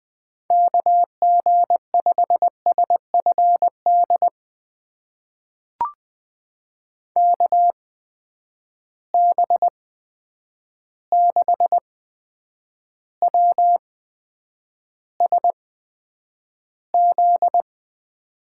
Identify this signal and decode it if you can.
Morse code, heard as a keyed tone, about 20 wpm, spaced as for 9 wpm, 700 Hz. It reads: KG5SFD KB6WSZ
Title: USA Call Signs - 6 Characters - 8x Character Spacing - 20wpm